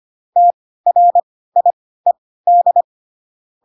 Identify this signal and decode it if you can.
Morse code, heard as a keyed tone, about 25 wpm, spaced as for 16 wpm, 700 Hz. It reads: TRIED
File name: Pinocchio / Pinocchio ch1 10 25 ik1hge